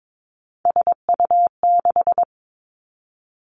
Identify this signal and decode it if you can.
Morse code, heard as a keyed tone, about 22 wpm, 700 Hz. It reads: SU6
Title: Three Challenging Letter-Number Combinations 22wpm